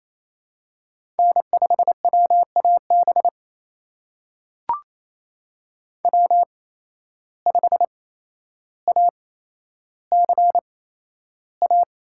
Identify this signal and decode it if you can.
Morse code, heard as a keyed tone, about 27 wpm, spaced as for 12 wpm, 700 Hz. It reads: N5WAB W5ACA
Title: USA Call Signs - 5 Characters - 8x Character Spacing - 28wpm